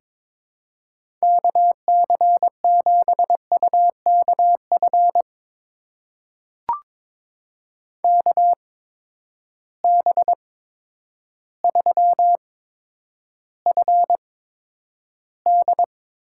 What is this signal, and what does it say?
Morse code, heard as a keyed tone, about 22 wpm, spaced as for 10 wpm, 700 Hz. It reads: KC7UKF KB3FD
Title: USA Call Signs - 6 Characters - 8x Character Spacing - 22wpm